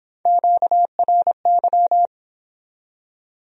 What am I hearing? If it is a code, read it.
Morse code, heard as a keyed tone, about 26 wpm, 700 Hz. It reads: QRY